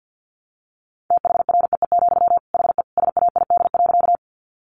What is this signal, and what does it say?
Morse code, heard as a keyed tone, about 48 wpm, 700 Hz. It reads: THREE, HE SAID.
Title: Sentences from Top 200 Words 50wpm